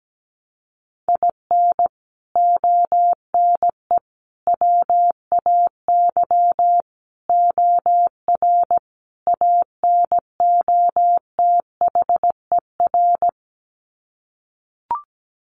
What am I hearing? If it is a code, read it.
Morse code, heard as a keyed tone, about 17 wpm, 700 Hz. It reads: IN ONE WAY OR ANOTHER
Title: Sets of 5 Words Encore 17wpm